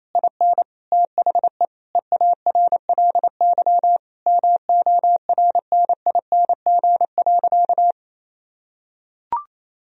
Morse code, heard as a keyed tone, about 28 wpm, 700 Hz. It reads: IN THE EARLY MORNING.
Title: Sentences from Top 1000 Words - Review Part 3 - 28wpm